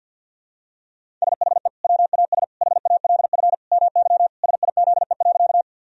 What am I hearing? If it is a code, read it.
Morse code, heard as a keyed tone, about 50 wpm, 700 Hz. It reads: SHE WAS HALF MY SIZE.
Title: Sentences from Top 1000 Words - Review Part 2 - 50wpm